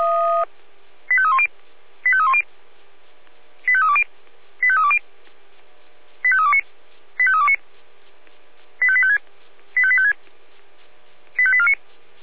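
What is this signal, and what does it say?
Five-tone selective call (ZVEI), heard as quick runs of five tones, tones of about 70 ms. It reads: 85310 85310 85320 85320 85329 85329 85856 85856 85849
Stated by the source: ZVEI five-tone selective call received with an SDR.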